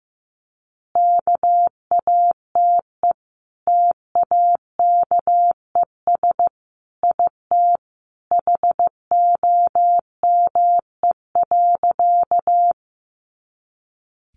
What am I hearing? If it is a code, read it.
Morse code, heard as a keyed tone, about 15 wpm, 700 Hz. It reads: KATE TAKES IT HOME.
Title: Easy Phrases 15wpm